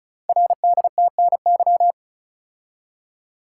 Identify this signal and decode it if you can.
Morse code, heard as a keyed tone, about 36 wpm, 700 Hz. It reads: RDTNY
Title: Code Groups - 5 Letters - 35wpm